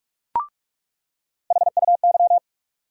Morse code, heard as a keyed tone, about 46 wpm, 700 Hz. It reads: HUY
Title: USA Call Sign Postfixes - 3 Letters 45wpm